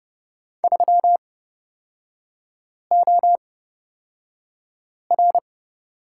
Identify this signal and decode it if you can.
Morse code, heard as a keyed tone, about 29 wpm, spaced as for 5 wpm, 690 Hz. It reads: 3OR
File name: Lesson 38e - Letter q - Three Characters 30wpm